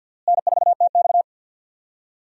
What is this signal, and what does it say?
Morse code, heard as a keyed tone, about 48 wpm, 700 Hz. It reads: N4TX
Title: USA Call Signs - 4 Characters - Repeated 3x - 50wpm